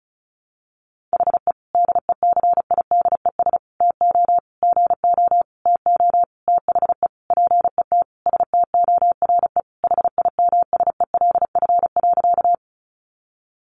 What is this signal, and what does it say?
Morse code, heard as a keyed tone, about 34 wpm, 700 Hz. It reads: HE DECIDES TO GO TO THE PET STORE HIMSELF.
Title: Additional Easy Phrases 35wpm